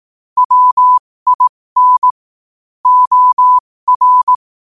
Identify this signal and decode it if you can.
Morse code, heard as a keyed tone, about 18 wpm, spaced as for 15 wpm, 1000 Hz. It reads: WIN OR
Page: 15 WPM morse code quotes for Tue, 23 Sep 2025.